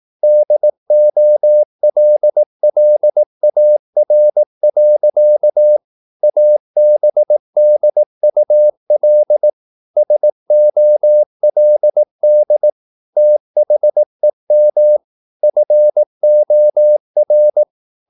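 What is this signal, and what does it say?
Morse code, heard as a keyed tone, about 18 wpm, 600 Hz. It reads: DOLLAR. ABDUL SOLD THEM FOR
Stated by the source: Morse practice at 18 wpm, English text, from CW med Gnister.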